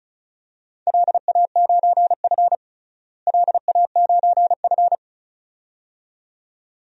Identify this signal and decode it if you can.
Morse code, heard as a keyed tone, about 36 wpm, 700 Hz. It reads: LA9F LA9F
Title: DXCC-100 Call Signs - 4 Characters - Repeated 2x - 35wpm